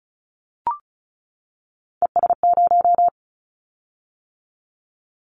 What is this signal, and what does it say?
Morse code, heard as a keyed tone, about 36 wpm, 700 Hz. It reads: ES0
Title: Callsign Prefixes for Top 100 DXCC - 35wpm